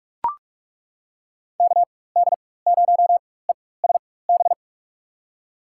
Morse code, heard as a keyed tone, about 42 wpm, spaced as for 21 wpm, 700 Hz. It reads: KD0ESB
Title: USA Call Signs - 6 Characters - 4x Character Spacing - 45wpm